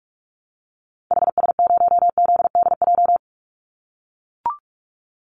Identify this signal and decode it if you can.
Morse code, heard as a keyed tone, about 44 wpm, 700 Hz. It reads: HS0ZDJ
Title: Call Signs - 45wpm